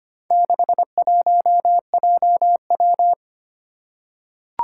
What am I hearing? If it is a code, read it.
Morse code, heard as a keyed tone, about 25 wpm, 700 Hz. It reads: 61JW
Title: Four Challenging Letter-Number Combinations 25wpm